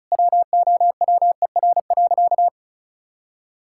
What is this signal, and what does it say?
Morse code, heard as a keyed tone, about 34 wpm, 700 Hz. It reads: WOWER.
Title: Pangram Challenge v2 - 35wpm